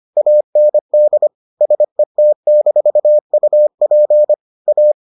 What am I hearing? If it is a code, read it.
Morse code, heard as a keyed tone, about 25 wpm, 600 Hz. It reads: AND SET-UP A